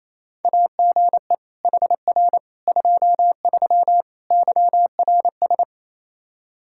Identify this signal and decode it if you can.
Morse code, heard as a keyed tone, about 28 wpm, 700 Hz. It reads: AGE HR 23 YRS
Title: Practice QSO - Exchange 3 - More Info - 28wpm